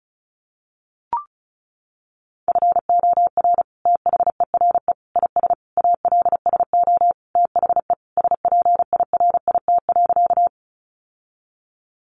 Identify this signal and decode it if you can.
Morse code, heard as a keyed tone, about 34 wpm, 700 Hz. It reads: FOR THERE IS ALSO THE SPIRIT.
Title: Sentences from Top 1400 Words 35wpm